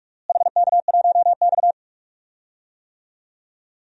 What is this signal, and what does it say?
Morse code, heard as a keyed tone, about 46 wpm, 700 Hz. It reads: HK1X
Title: DXCC-100 Call Signs - 4 Characters - Repeated 3x - 45wpm